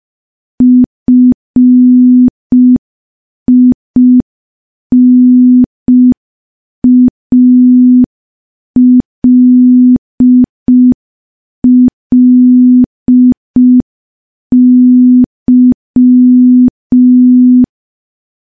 Morse code, heard as a keyed tone, about 5 wpm, 250 Hz. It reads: FINALLY
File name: morse / morse-training-05WPM